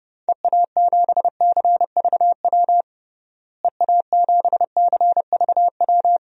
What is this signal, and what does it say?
Morse code, heard as a keyed tone, about 32 wpm, 700 Hz. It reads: EA7CVW EA7CVW
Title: DXCC-100 Call Signs - 6 Characters - Repeated 3x - 30wpm